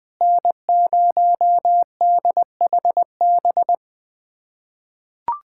Morse code, heard as a keyed tone, about 20 wpm, 700 Hz. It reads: N0DHB
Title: USA Callsign DE Callsign - 5 Characters - 20wpm